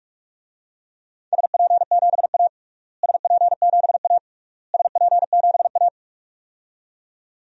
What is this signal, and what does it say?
Morse code, heard as a keyed tone, about 47 wpm, 700 Hz. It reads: SP7A SP7A SP7A
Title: DXCC-100 Call Signs - 4 Characters - Repeated 3x - 45wpm